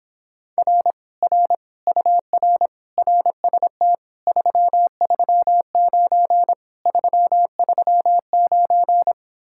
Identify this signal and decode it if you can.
Morse code, heard as a keyed tone, about 26 wpm, 700 Hz. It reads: R R UR RST 339 339